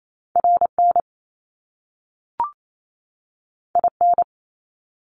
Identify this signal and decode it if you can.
Morse code, heard as a keyed tone, about 28 wpm, 700 Hz. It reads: RN IN
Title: Top Two Letter Combinations in Words 28wpm